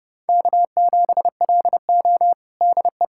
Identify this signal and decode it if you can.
Morse code, heard as a keyed tone, about 30 wpm, 700 Hz. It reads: K7LO DE
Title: USA Callsign DE Callsign - 4 Characters - 30wpm